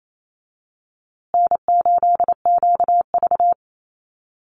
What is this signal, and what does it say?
Morse code, heard as a keyed tone, about 28 wpm, 700 Hz. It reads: N8QV
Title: USA Call Signs Repeated 2x 28wpm